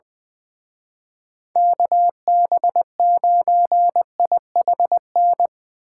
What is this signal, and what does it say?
Morse code, heard as a keyed tone, about 22 wpm, 700 Hz. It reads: KB9IHN
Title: Speed Racing - USA 2x3 Call Signs 20wpm